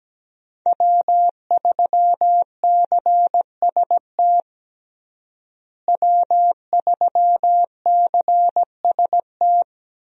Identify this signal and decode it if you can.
Morse code, heard as a keyed tone, about 17 wpm, 700 Hz. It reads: W3CST W3CST
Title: USA Call Signs - 5 Characters - Repeated 2x - 17wpm